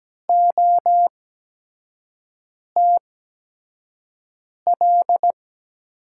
Morse code, heard as a keyed tone, about 17 wpm, spaced as for 5 wpm, 700 Hz. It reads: OTL
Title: USA Call Signs - 8x Character Spacing - 17wpm